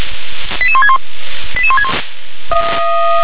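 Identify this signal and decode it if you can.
Five-tone selective call (ZVEI), heard as quick runs of five tones, tones of about 80 ms. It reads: 88161 88161
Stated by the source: ZVEI five-tone selective call 88126, received on SDR.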